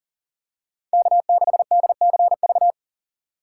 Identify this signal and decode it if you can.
Morse code, heard as a keyed tone, about 40 wpm, 700 Hz. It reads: K6DCV
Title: USA Call Signs - 5 Characters - 6x Character Spacing - 40wpm